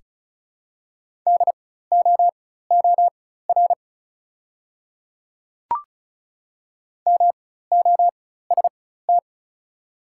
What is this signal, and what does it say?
Morse code, heard as a keyed tone, about 36 wpm, spaced as for 16 wpm, 690 Hz. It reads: DOOR MOST
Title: 4x Character Spacing - Top 300 Words - 35wpm